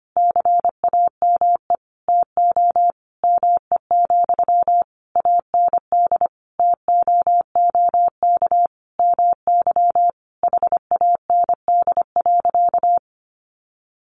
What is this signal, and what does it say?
Morse code, heard as a keyed tone, about 25 wpm, 690 Hz. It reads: CAME TO ME, AND TOOK MY HAND.
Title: Sentences from Top 1100 Words 25wpm